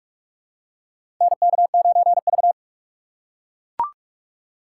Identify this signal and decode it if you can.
Morse code, heard as a keyed tone, about 44 wpm, 700 Hz. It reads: NK9V
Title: USA Call Signs - 4 Characters - 6x Character Spacing - 45wpm